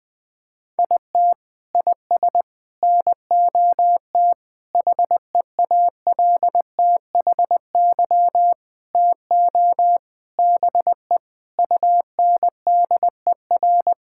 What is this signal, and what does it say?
Morse code, heard as a keyed tone, about 20 wpm, 700 Hz. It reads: IT IS NOT HEALTHY TO BE UNDER
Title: Sentences from Top 1000 Words - Review Part 6 - 20wpm